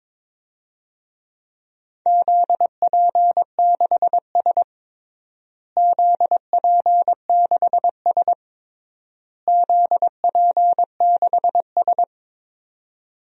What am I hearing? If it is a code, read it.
Morse code, heard as a keyed tone, about 22 wpm, 700 Hz. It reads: ZP6S ZP6S ZP6S
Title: DXCC-100 Call Signs - 4 Characters - Repeated 3x - 22wpm